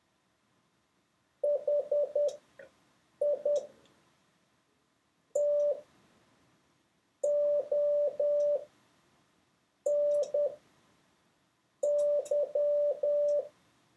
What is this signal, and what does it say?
Morse code, heard as a keyed tone, about 10 wpm, spaced as for 5 wpm, 580 Hz. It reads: HI TONY
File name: 6of6-test